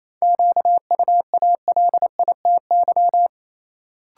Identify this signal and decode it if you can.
Morse code, heard as a keyed tone, about 28 wpm, 700 Hz. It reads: QUALITY
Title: Sentences from Top 1000 Words - Review Part 4 - New Word Drill - 28wpm